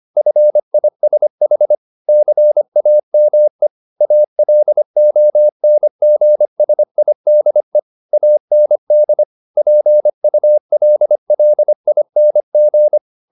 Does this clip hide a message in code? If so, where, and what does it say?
Yes: Morse code from 0.2 to 13.0 s, heard as a keyed tone, about 25 wpm, 600 Hz. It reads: FISH CAME ALONGSIDE AND PULLING